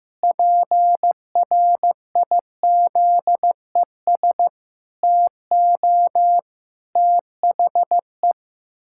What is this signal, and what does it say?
Morse code, heard as a keyed tone, about 15 wpm, 700 Hz. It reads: PRIZES TO THE
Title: Pangram Challenge v2 - 15wpm